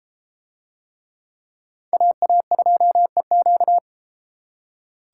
Morse code, heard as a keyed tone, about 31 wpm, 700 Hz. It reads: AA2EQ